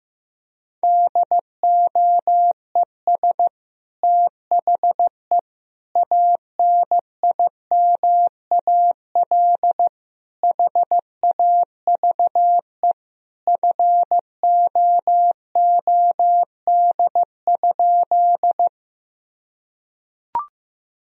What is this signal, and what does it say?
Morse code, heard as a keyed tone, about 15 wpm, 700 Hz. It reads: DOES THE ANIMAL HAVE FOOD?
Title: Sentences from Top 1000 Words - Review Part 1 - 15wpm